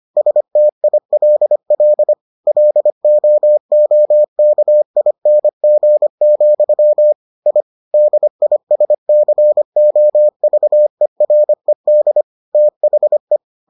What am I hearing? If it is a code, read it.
Morse code, heard as a keyed tone, about 25 wpm, 600 Hz. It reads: STILL LOOKING, I DISCOVERED THE